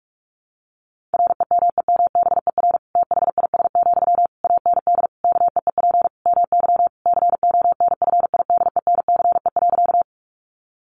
Morse code, heard as a keyed tone, about 44 wpm, 700 Hz. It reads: REMEMBER THIS, AND KEEP MY CONFIDENCE.
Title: Sentences from Top 1500 Words 45wpm